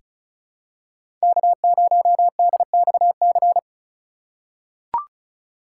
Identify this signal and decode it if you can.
Morse code, heard as a keyed tone, about 34 wpm, 700 Hz. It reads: K0DXC
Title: K1USN SST QSO Practice - 35wpm